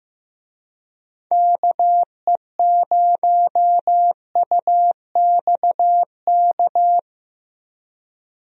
Morse code, heard as a keyed tone, about 15 wpm, 700 Hz. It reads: KE0UXK
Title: USA Call Signs - 6 Characters - 15wpm